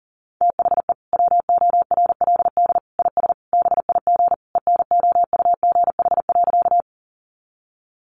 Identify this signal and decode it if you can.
Morse code, heard as a keyed tone, about 40 wpm, 700 Hz. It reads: THE WORLD IS BIG ENOUGH.
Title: Sentences from Top 400 Words 40wpm